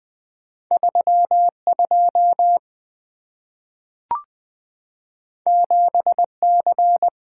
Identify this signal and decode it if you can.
Morse code, heard as a keyed tone, about 20 wpm, 700 Hz. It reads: 32 7C
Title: Two Letter-Number Combinations - 20wpm